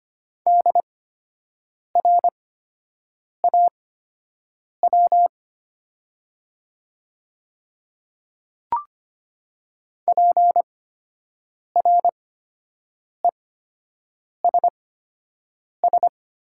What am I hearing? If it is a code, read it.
Morse code, heard as a keyed tone, about 25 wpm, spaced as for 7 wpm, 700 Hz. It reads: DRAW PRESS
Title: 8x Character Spacing - Top 500 Words - 25wpm